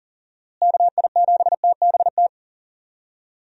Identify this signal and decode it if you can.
Morse code, heard as a keyed tone, about 40 wpm, 700 Hz. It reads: KI7TBT
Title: USA Call Signs - 6 Characters - Repeated 3x - 40wpm